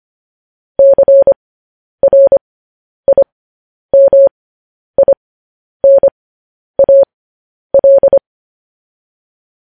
Morse code, heard as a keyed tone, about 25 wpm, spaced as for 10 wpm, 550 Hz. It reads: CRIMINAL